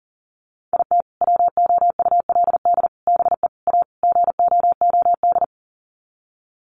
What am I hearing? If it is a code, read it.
Morse code, heard as a keyed tone, about 37 wpm, 700 Hz. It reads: IT WOULD BE A GOOD
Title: Sets of 5 Words Encore2 40wpm